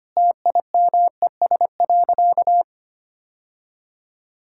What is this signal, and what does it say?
Morse code, heard as a keyed tone, about 25 wpm, 700 Hz. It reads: TIMES.
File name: practice / 25wpm / Sentences with 4-Word Phrases - 25wpm